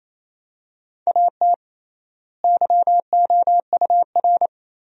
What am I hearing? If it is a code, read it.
Morse code, heard as a keyed tone, about 27 wpm, 700 Hz. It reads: AT YOUR
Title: 3x Word-Spacing - Sentences from Top 200 Words - 28wpm